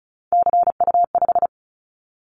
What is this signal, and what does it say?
Morse code, heard as a keyed tone, about 36 wpm, 700 Hz. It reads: CU5
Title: Callsign Prefixes for Top 50 DXCC - 35wpm